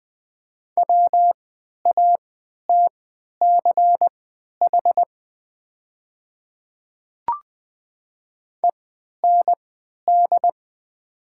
Morse code, heard as a keyed tone, about 20 wpm, spaced as for 11 wpm, 700 Hz. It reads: WATCH END
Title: 3x Character Spacing - Top 300 Words - 20wpm